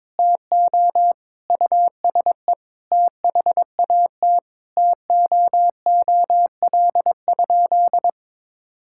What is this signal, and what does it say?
Morse code, heard as a keyed tone, about 22 wpm, 700 Hz. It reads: TO USE THAT TOOL?
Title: Sentences from Top 1000 Words - Review Part 6 - 22wpm